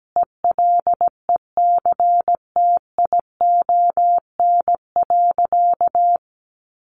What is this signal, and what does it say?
Morse code, heard as a keyed tone, about 17 wpm, 700 Hz. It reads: ELECTION.
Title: Sentences from Top 1500 Words 17wpm